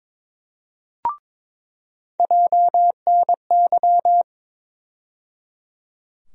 Morse code, heard as a keyed tone, about 22 wpm, 700 Hz. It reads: JNY
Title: USA Call Sign Postfixes - 3 Letters 22wpm